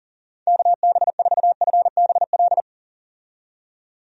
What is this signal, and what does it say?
Morse code, heard as a keyed tone, about 40 wpm, 700 Hz. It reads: KB4FBL
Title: USA Call Signs - 6 Characters - 40wpm